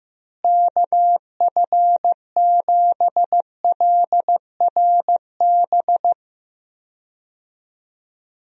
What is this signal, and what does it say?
Morse code, heard as a keyed tone, about 16 wpm, 700 Hz. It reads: KF7LRB
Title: USA Call Signs - 6 Characters - Repeated 2x - 15wpm